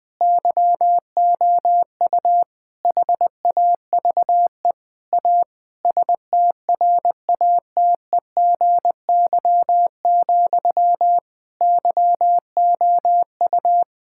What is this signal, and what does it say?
Morse code, heard as a keyed tone, about 20 wpm, 700 Hz. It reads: YOU HAVE A STRATEGY, YOU
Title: Sentences from Top 1000 Words - Review Part 5 - 20wpm